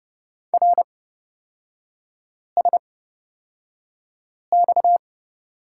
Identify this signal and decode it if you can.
Morse code, heard as a keyed tone, about 30 wpm, spaced as for 5 wpm, 700 Hz. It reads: RSX
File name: Lesson 37e - Letter x - Three Characters 30wpm